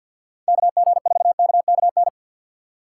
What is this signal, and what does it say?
Morse code, heard as a keyed tone, about 52 wpm, 700 Hz. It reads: KG4KKN
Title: USA Call Signs - 6 Characters - 3x Character Spacing - 50wpm